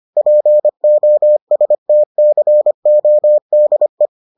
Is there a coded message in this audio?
Yes: Morse code from 0.2 to 4.1 s, heard as a keyed tone, about 25 wpm, 600 Hz. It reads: POSTCODE